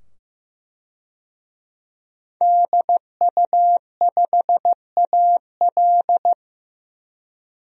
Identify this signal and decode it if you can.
Morse code, heard as a keyed tone, about 16 wpm, 700 Hz. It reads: DU5AL
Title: DXCC-100 Call Signs - 5 Characters - 15wpm